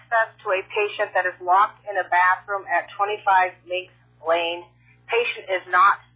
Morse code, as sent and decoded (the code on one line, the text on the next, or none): none